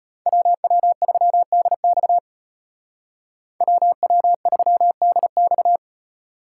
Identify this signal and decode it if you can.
Morse code, heard as a keyed tone, about 33 wpm, 700 Hz. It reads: WW3DX WW3DX